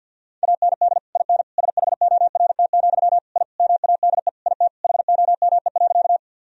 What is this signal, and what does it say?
Morse code, heard as a keyed tone, about 49 wpm, 700 Hz. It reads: AND IN SHORT, I MADE IT HOME.
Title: Sentences from Top 1000 Words - Review Part 2 - 50wpm